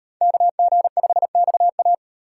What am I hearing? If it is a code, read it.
Morse code, heard as a keyed tone, about 38 wpm, 700 Hz. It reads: KG5XA